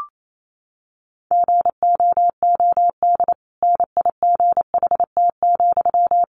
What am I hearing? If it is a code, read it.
Morse code, heard as a keyed tone, about 28 wpm, 700 Hz. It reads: GOOD NIGHT,
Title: Sentences from Top 1500 Words 28wpm